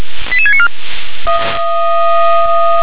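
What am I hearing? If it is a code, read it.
Five-tone selective call (ZVEI), heard as quick runs of five tones, tones of about 70 ms. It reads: 88694
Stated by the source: ZVEI five-tone selective call 88711 received by SDR.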